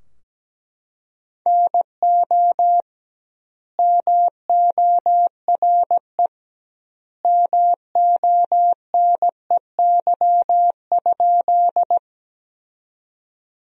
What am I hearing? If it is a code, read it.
Morse code, heard as a keyed tone, about 17 wpm, 700 Hz. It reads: NO MORE MONEY?
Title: 2x Word-Spacing - Sentences from Top 400 Words - 17wpm